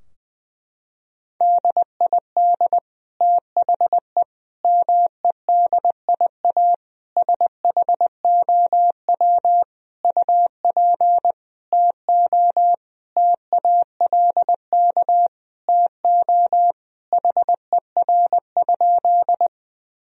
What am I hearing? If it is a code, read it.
Morse code, heard as a keyed tone, about 20 wpm, 700 Hz. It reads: DID THE MEDIA SHOW UP TO TALK TO HER?
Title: Sentences from Top 1000 Words - Review Part 4 - 20wpm